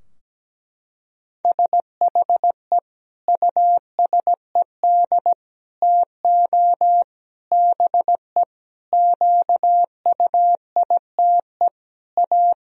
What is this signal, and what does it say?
Morse code, heard as a keyed tone, about 17 wpm, 700 Hz. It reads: SHE USED TO BE QUITE A
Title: Sentences from Top 1000 Words - Review Part 2 - 17wpm